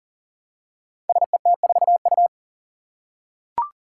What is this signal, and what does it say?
Morse code, heard as a keyed tone, about 38 wpm, 700 Hz. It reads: SET4U
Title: Five Challenging Letter-Number Combinations 40wpm